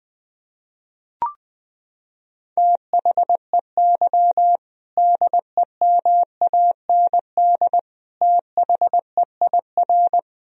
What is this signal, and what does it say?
Morse code, heard as a keyed tone, about 19 wpm, 700 Hz. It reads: THEY DEMAND THEIR
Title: Sentences from Top 1000 Words - Review Part 3 - 20wpm